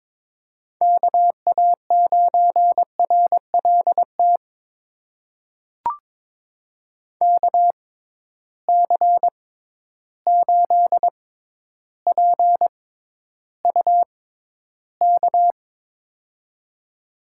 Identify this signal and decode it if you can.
Morse code, heard as a keyed tone, about 22 wpm, spaced as for 11 wpm, 700 Hz. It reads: KA9RLT KC8PUK
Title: USA Call Signs - 6 Characters - 6x Character Spacing - 22wpm